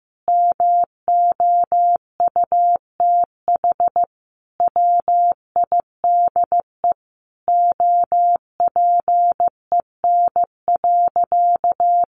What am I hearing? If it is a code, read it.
Morse code, heard as a keyed tone, about 15 wpm, 700 Hz. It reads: MOUTH WIDE OPEN.
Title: Sentences from Top 1500 Words 15wpm